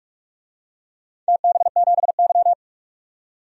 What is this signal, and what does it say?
Morse code, heard as a keyed tone, about 46 wpm, 700 Hz. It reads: TB7Y
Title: DXCC-100 Call Signs - 4 Characters - 45wpm